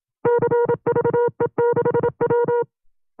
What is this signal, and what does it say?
Morse code, heard as a keyed tone, about 27 wpm, 480 Hz. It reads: CVE6W
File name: FreeFiles / CW-practice / 25WPM-side2